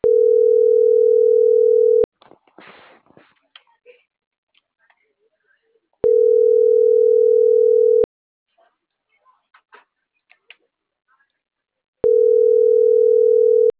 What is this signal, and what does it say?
Ringback tone (North America), 440 + 480 Hz.
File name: CENTRAL MAERSK / out martin.blas